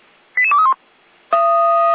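Five-tone selective call (ZVEI), heard as quick runs of five tones, tones of about 70 ms. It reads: 88231
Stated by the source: ZVEI five-tone selective call received on SDR.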